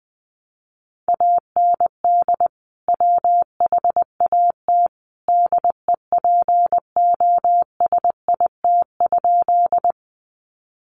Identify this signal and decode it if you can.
Morse code, heard as a keyed tone, about 20 wpm, 700 Hz. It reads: AND WHAT DEPOSIT?
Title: Sentences from Top 1300 Words 20wpm